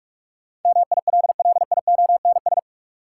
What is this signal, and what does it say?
Morse code, heard as a keyed tone, about 46 wpm, 700 Hz. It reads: MILLIONS
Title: Sets of 5 Words - New Word Drill - 45wpm